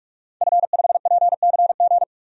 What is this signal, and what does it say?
Morse code, heard as a keyed tone, about 46 wpm, 700 Hz. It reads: F5PCG
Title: DXCC-100 Call Signs - 5 Characters - 45wpm